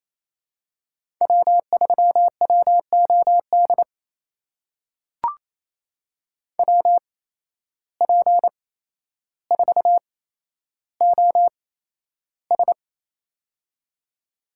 Morse code, heard as a keyed tone, about 27 wpm, spaced as for 12 wpm, 700 Hz. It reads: W3WOD WP4OS
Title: USA Call Signs - 5 Characters - 8x Character Spacing - 28wpm